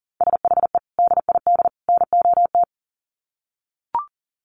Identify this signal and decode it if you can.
Morse code, heard as a keyed tone, about 40 wpm, 700 Hz. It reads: SHE DID NOT
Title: Sets of 3 Words 40wpm